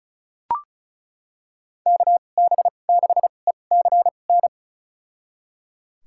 Morse code, heard as a keyed tone, about 34 wpm, spaced as for 25 wpm, 700 Hz. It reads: KB6ECN
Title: USA Call Signs - 6 Characters - 2x Character Spacing - 35wpm